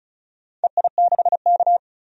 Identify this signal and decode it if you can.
Morse code, heard as a keyed tone, about 36 wpm, 700 Hz. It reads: EI6K
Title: DXCC-100 Call Signs - 4 Characters - 35wpm